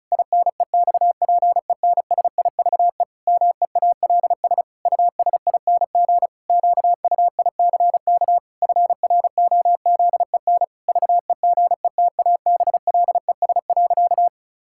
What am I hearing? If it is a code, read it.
Morse code, heard as a keyed tone, about 36 wpm, 700 Hz. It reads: INEXPENSIVE MEALS USING QUICK FROZEN VEGETABLES.
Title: Pangram Challenge v2 - 35wpm